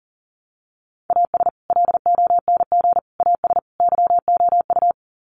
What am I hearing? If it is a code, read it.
Morse code, heard as a keyed tone, about 40 wpm, 700 Hz. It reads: AS LONG AS YOU
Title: Sets of 4 Words 40wpm